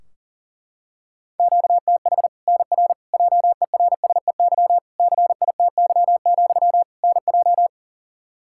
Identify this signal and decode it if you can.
Morse code, heard as a keyed tone, about 40 wpm, 700 Hz. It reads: QTH NR JERSEY CITY, NJ
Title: K1USN SST QSO Practice 40wpm